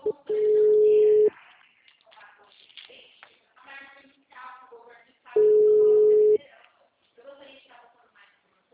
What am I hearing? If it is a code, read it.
Ringback tone (Europe, 425 Hz), with speech.
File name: out- -2003- - - .13013